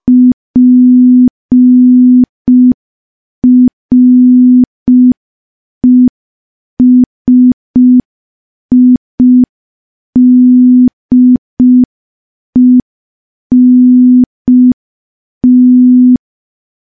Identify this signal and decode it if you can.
Morse code, heard as a keyed tone, about 5 wpm, 250 Hz. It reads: PRESIDENT
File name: morse / morse-training-05WPM